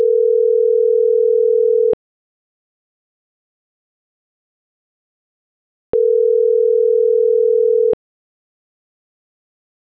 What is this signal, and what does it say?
Ringback tone (North America), 440 + 480 Hz.